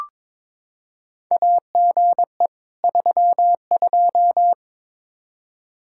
Morse code, heard as a keyed tone, about 22 wpm, 700 Hz. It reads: AGE 32
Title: K1USN SST QSO Practice 22wpm